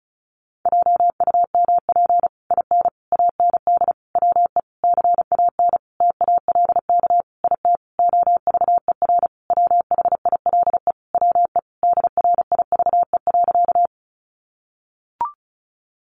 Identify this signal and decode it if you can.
Morse code, heard as a keyed tone, about 34 wpm, 700 Hz. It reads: JUMP IN AND WE CAN TALK IT OVER WHILE WE DRIVE.
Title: Sentences from Top 1300 Words 35wpm